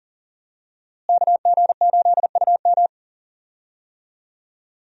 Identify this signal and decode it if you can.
Morse code, heard as a keyed tone, about 40 wpm, 700 Hz. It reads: KG8UM
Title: USA Call Signs - 5 Characters - Repeated 2x - 40wpm